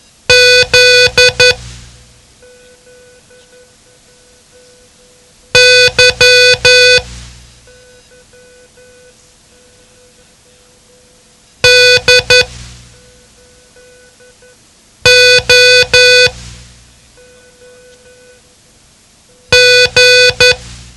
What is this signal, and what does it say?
Morse code, heard as a keyed tone, about 11 wpm, spaced as for 2 wpm, 500 Hz. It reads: ZY DOG
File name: ham radio / CodeQuick / Code Quick 07